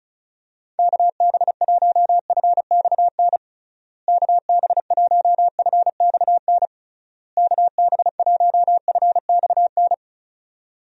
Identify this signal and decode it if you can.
Morse code, heard as a keyed tone, about 34 wpm, 700 Hz. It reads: KB1FXN KB1FXN KB1FXN
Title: USA Call Signs - 6 Characters - Repeated 3x - 35wpm